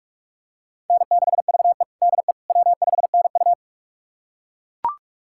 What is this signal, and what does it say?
Morse code, heard as a keyed tone, about 41 wpm, 700 Hz. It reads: N6VE DE W5NU
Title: USA Callsign DE Callsign - 4 Characters - 45wpm